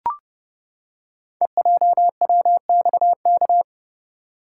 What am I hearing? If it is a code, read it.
Morse code, heard as a keyed tone, about 29 wpm, 700 Hz. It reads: EJWXK
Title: Code Groups - 5 Letters - 30wpm